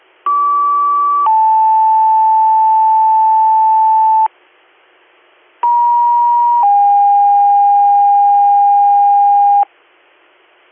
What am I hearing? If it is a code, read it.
Two-tone page, heard as a one-second tone, then a three-second tone, about 1150 then 880 Hz, about 950 then 810 Hz. It is heard from 0.3 to 9.6 s.